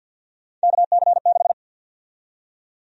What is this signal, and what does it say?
Morse code, heard as a keyed tone, about 53 wpm, 700 Hz. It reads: KC6
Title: USA Call Sign Prefixes - 50wpm